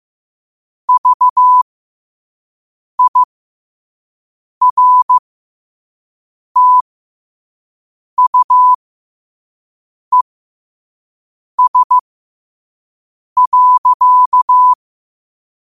Morse code, heard as a keyed tone, about 15 wpm, spaced as for 5 wpm, 1000 Hz. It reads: VIRTUES.
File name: qotdfiles / 5wpm